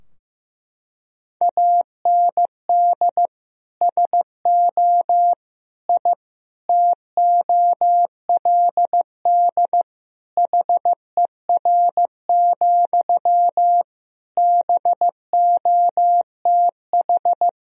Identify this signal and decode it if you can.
Morse code, heard as a keyed tone, about 15 wpm, 700 Hz. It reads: AND SO I TOLD HER, BOTH
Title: Sentences from Top 1000 Words - Review Part 4 - 15wpm